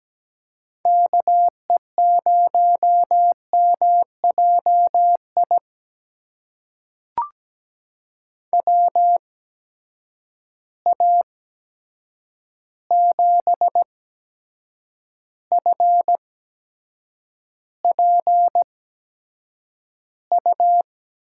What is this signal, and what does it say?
Morse code, heard as a keyed tone, about 17 wpm, spaced as for 7 wpm, 700 Hz. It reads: KE0MJI WA7FPU
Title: USA Call Signs - 6 Characters - 8x Character Spacing - 17wpm